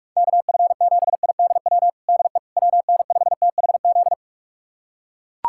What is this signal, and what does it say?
Morse code, heard as a keyed tone, about 41 wpm, 700 Hz. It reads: KF7IDW DE WN5THZ
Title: USA Callsign DE Callsign - 6 Characters - 45wpm